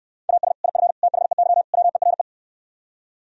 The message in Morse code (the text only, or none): AS IF THEY WERE